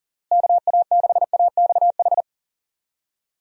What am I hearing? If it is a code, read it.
Morse code, heard as a keyed tone, about 40 wpm, 700 Hz. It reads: KA6AXH
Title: USA Call Signs - 6 Characters - Repeated 3x - 40wpm